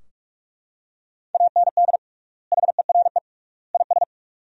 Morse code, heard as a keyed tone, about 43 wpm, 700 Hz. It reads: AND HERE IS
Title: 3x Word-Spacing - Sentences from Top 500 Words - 45wpm